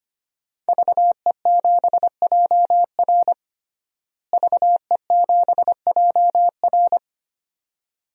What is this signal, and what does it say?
Morse code, heard as a keyed tone, about 25 wpm, 700 Hz. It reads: VE7JR VE7JR
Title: DXCC-100 Call Signs - 5 Characters - Repeated 2x - 25wpm